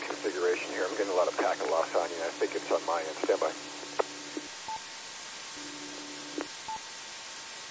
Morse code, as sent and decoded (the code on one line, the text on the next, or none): none